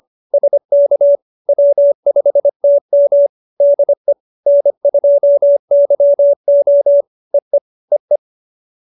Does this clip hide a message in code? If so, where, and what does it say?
Yes: Morse code from 0.3 to 8.2 s, heard as a keyed tone, about 25 wpm, 570 Hz. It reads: SK W5TM DE N2YO EE EE